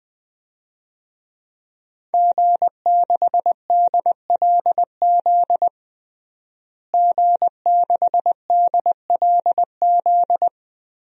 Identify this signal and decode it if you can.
Morse code, heard as a keyed tone, about 20 wpm, 700 Hz. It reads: G6DLZ G6DLZ
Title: DXCC-100 Call Signs - 5 Characters - Repeated 3x - 20wpm